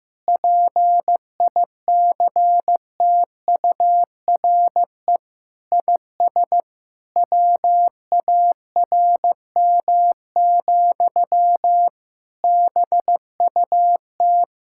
Morse code, heard as a keyed tone, about 15 wpm, 700 Hz. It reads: PICTURE IS WARM, BUT
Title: Sentences from Top 1600 Words 15wpm